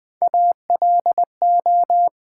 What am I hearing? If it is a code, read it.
Morse code, heard as a keyed tone, about 20 wpm, 700 Hz. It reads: ALO
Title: Build A Word - 1 Letter At A Time - 3 to 5-Letter Words - 20wpm